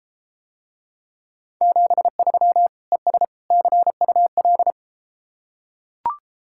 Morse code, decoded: 73 ES CUL

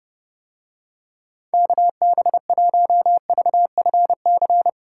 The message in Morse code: KB1VFC

